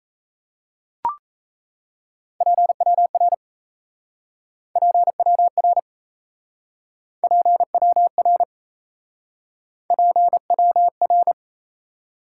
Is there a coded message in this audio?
Yes: Morse code from 2.4 to 11.3 s, heard as a keyed tone, about 38 wpm, 700 Hz. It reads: PWR PWR PWR PWR